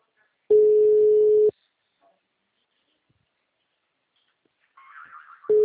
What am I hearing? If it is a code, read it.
Ringback tone (Europe, 425 Hz).